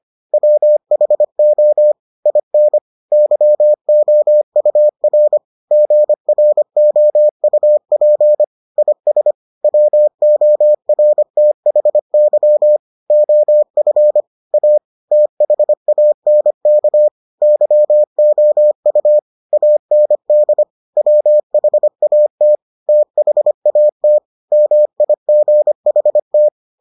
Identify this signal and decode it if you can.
Morse code, heard as a keyed tone, about 25 wpm, 600 Hz. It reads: WHO IN YOUR GROUP IS WORTHY OF A THANK YOU AND WHAT THAT MIGHT